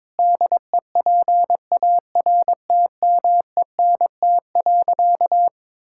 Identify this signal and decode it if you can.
Morse code, heard as a keyed tone, about 21 wpm, 700 Hz. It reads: DEPARTMENT.
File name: practice / 22wpm / Sentences with 3-Word Phrases - 22wpm